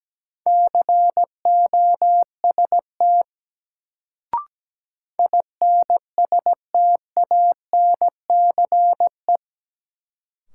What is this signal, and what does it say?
Morse code, heard as a keyed tone, about 17 wpm, 700 Hz. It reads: COST INSTANCE
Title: Sentences from Top 800 Words - New Word Drill - 17wpm